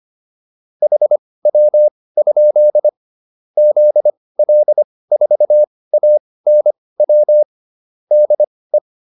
Morse code, heard as a keyed tone, about 25 wpm, spaced as for 18 wpm, 600 Hz. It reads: HW? ZL4ANW DE